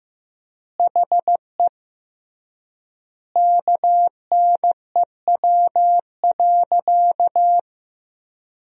Morse code, heard as a keyed tone, about 15 wpm, 700 Hz. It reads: HE KNEW.
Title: 3x Word-Spacing - Sentences from Top 400 Words - 15wpm